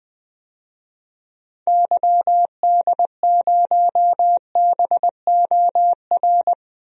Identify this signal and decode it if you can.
Morse code, heard as a keyed tone, about 20 wpm, 700 Hz. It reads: YD0BOR
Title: DXCC-100 Call Signs - 6 Characters - 20wpm